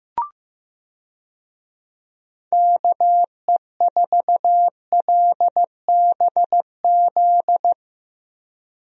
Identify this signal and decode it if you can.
Morse code, heard as a keyed tone, about 15 wpm, 700 Hz. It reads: KE4LBZ
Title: USA Call Signs - 6 Characters - 15wpm